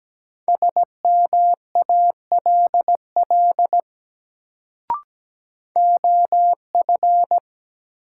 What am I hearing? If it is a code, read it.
Morse code, heard as a keyed tone, about 17 wpm, 700 Hz. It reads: SMALL OF
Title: Sets of 2 Words - New Word Drill - 17wpm